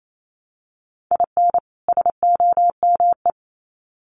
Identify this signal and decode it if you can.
Morse code, heard as a keyed tone, about 28 wpm, 700 Hz. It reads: IN SOME
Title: Sets of 2 Words 28wpm